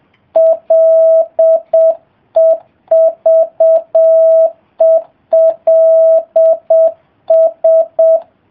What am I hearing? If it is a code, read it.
Morse code, heard as a keyed tone, about 7 wpm, 660 Hz. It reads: LEVELS